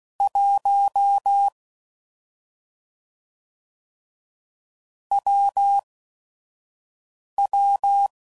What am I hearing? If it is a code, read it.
Morse code, heard as a keyed tone, about 16 wpm, spaced as for 5 wpm, 800 Hz. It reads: 1 WW
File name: Combinations / Difficult Combos / AWJ1